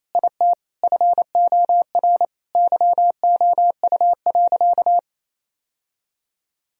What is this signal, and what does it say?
Morse code, heard as a keyed tone, about 27 wpm, 700 Hz. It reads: IT FOR YOU.
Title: Sentences from Top 1000 Words - Review Part 1 - 28wpm